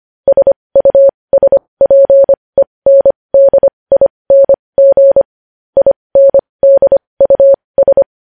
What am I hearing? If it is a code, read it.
Morse code, heard as a keyed tone, about 25 wpm, spaced as for 20 wpm, 550 Hz. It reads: SUSPENDING INDUS
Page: news headlines in morse code at 20 wpm